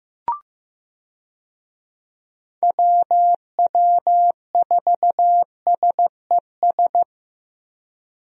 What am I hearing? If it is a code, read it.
Morse code, heard as a keyed tone, about 15 wpm, 700 Hz. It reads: WW4SES
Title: USA Call Signs - 6 Characters - Repeated 2x - 15wpm